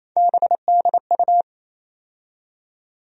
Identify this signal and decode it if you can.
Morse code, heard as a keyed tone, about 28 wpm, 700 Hz. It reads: BDU
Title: USA Call Sign Postfixes - 3 Letters 28wpm